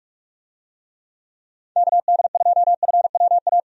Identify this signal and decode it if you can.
Morse code, heard as a keyed tone, about 46 wpm, 700 Hz. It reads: KD2FWA
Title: USA Call Signs - 6 Characters - 45wpm